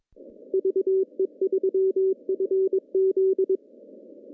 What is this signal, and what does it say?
Morse code, heard as a keyed tone, about 20 wpm, 400 Hz. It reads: VE3FZ
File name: CW / cqww-cw-k5tr